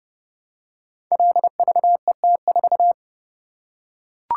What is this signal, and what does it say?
Morse code, heard as a keyed tone, about 30 wpm, 700 Hz. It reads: LVET4
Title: Code Groups - 5 Letter:Numbers - 30wpm